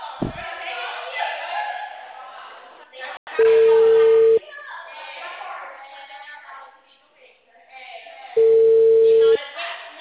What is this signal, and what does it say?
Ringback tone (China, 450 Hz), with speech.